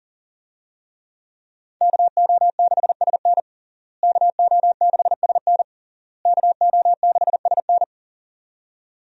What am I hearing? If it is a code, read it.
Morse code, heard as a keyed tone, about 40 wpm, 700 Hz. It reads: KO6SN KO6SN KO6SN
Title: USA Call Signs - 5 Characters - Repeated 3x - 40wpm